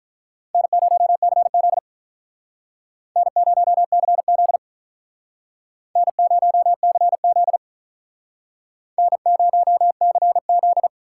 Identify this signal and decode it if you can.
Morse code, heard as a keyed tone, about 43 wpm, 700 Hz. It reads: N0CZ N0CZ N0CZ N0CZ
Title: Speed Racing - USA 1x2 Call Signs 35wpm